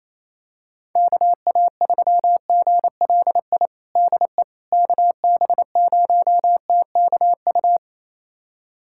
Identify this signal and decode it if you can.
Morse code, heard as a keyed tone, about 28 wpm, 700 Hz. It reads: KA3GLI DE KB0TKU
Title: USA Callsign DE Callsign - 6 Characters - 28wpm